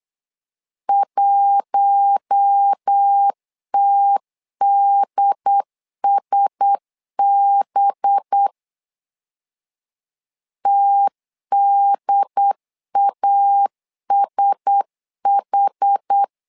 Morse code, heard as a keyed tone, about 8 wpm, 800 Hz. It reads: 1TDSB TDASH
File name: LEZIONI / 5 LBVC / 5 LB 8wpm 2816